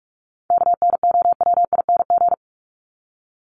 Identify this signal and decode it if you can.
Morse code, heard as a keyed tone, about 44 wpm, 700 Hz. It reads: KNOWING
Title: ING Suffix - Top 100 Words with Base Words 45wpm